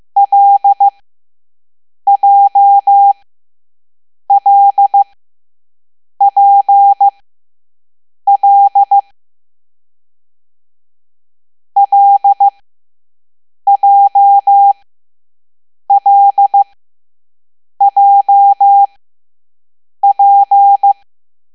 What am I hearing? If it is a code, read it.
Morse code, heard as a keyed tone, about 15 wpm, spaced as for 6 wpm, 800 Hz. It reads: LJLPL LJLJP